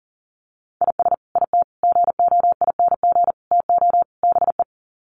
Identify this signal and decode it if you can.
Morse code, heard as a keyed tone, about 40 wpm, 700 Hz. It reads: IS IT GOING TO BE
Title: Sets of 5 Words Encore2 40wpm